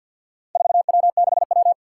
Morse code, heard as a keyed tone, about 51 wpm, 700 Hz. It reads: 4W6W
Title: Four Challenging Letter-Number Combinations 50wpm